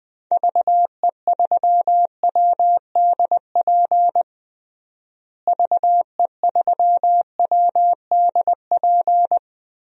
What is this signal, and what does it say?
Morse code, heard as a keyed tone, about 21 wpm, 700 Hz. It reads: VE3WDP VE3WDP
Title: DXCC-100 Call Signs - 6 Characters - Repeated 3x - 20wpm